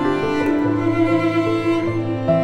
There is no code.